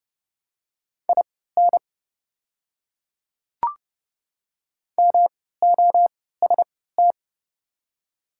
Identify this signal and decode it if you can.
Morse code, heard as a keyed tone, about 30 wpm, spaced as for 17 wpm, 700 Hz. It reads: IN MOST
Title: 3x Character Spacing - Top 100 Words - 30wpm